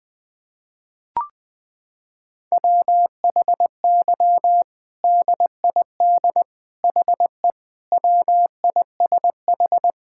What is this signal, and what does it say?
Morse code, heard as a keyed tone, about 20 wpm, 700 Hz. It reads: WHY DID HE WISH
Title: Sentences from Top 1600 Words 20wpm